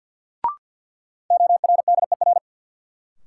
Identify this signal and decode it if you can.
Morse code, heard as a keyed tone, about 52 wpm, 700 Hz. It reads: ORDER
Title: Sets of 2 Words - New Word Drill - 50wpm